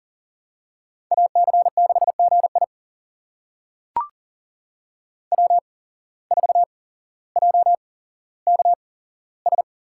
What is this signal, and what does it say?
Morse code, heard as a keyed tone, about 40 wpm, spaced as for 17 wpm, 700 Hz. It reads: AC6GI W4JKS
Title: USA Call Signs - 5 Characters - 8x Character Spacing - 40wpm